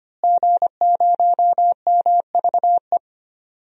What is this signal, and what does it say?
Morse code, heard as a keyed tone, about 25 wpm, 700 Hz. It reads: G0MVE
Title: DXCC-100 Call Signs - 5 Characters - Repeated 3x - 25wpm